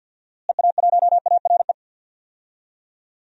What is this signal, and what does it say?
Morse code, heard as a keyed tone, about 47 wpm, 700 Hz. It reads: EA1ARE
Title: DXCC-100 Call Signs - 6 Characters - Repeated 2x - 50wpm